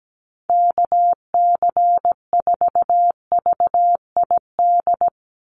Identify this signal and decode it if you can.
Morse code, heard as a keyed tone, about 17 wpm, 700 Hz. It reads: KC4VID
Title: USA Call Signs 17wpm